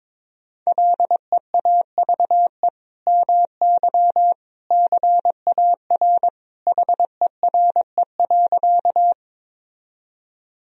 Morse code, heard as a keyed tone, about 22 wpm, 700 Hz. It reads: LEAVE MY CAR HERE.
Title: Sentences from Top 1000 Words - Review Part 1 - 22wpm